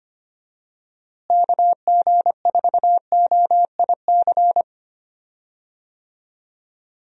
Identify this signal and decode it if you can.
Morse code, heard as a keyed tone, about 25 wpm, 700 Hz. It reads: KG4OIC
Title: USA Call Signs - 6 Characters - Repeated 2x - 25wpm